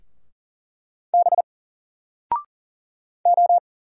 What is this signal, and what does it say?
Morse code, heard as a keyed tone, about 40 wpm, 700 Hz. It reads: B O
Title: Single Letter-Number 40wpm